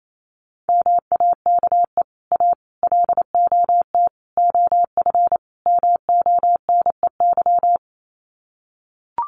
Morse code, heard as a keyed tone, about 28 wpm, 700 Hz. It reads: MAKE A LOT OF MONEY
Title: Sets of 5 Words Encore2 28wpm